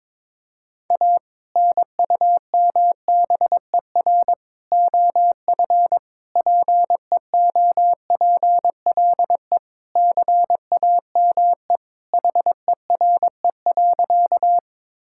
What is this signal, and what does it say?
Morse code, heard as a keyed tone, about 21 wpm, 700 Hz. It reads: A NUMBER OF PEOPLE CAME HERE.
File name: practice / 22wpm / Sentences with 2-Word Phrases - 22wpm